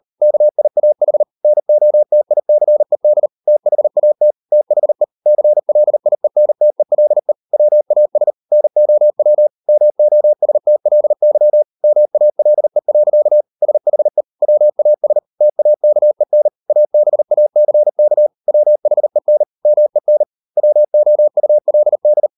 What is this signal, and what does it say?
Morse code, heard as a keyed tone, about 38 wpm, 600 Hz. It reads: KIAH NOTICED THAT THE CLIENTELE WAS NOW MOSTLY MALE. SHE WAS TAKEN ABACK WHEN MEN WOULD